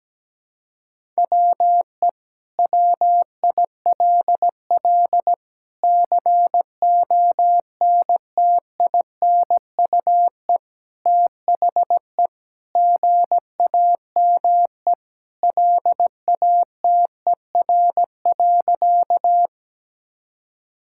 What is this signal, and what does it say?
Morse code, heard as a keyed tone, about 17 wpm, 700 Hz. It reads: WE WILL CONTINUE THE GAME LATER.
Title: Sentences from Top 1000 Words - Review Part 6 - 17wpm